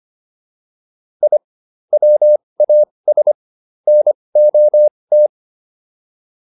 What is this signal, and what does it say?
Morse code, heard as a keyed tone, about 25 wpm, spaced as for 20 wpm, 600 Hz. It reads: I WAS NOT
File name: WD202 20